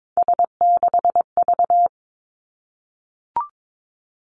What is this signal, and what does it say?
Morse code, heard as a keyed tone, about 22 wpm, 700 Hz. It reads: S6V
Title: Three Challenging Letter-Number Combinations 22wpm